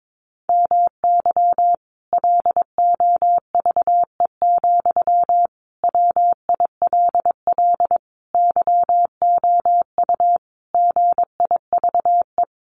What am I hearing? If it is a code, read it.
Morse code, heard as a keyed tone, about 22 wpm, 700 Hz. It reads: MY LOVE, WILL YOU GIVE
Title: Sentences from Top 1200 Words 22wpm